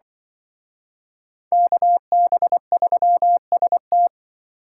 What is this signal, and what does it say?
Morse code, heard as a keyed tone, about 24 wpm, 700 Hz. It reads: KB3ST
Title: Speed Racing - USA 2x2 Call Signs 20wpm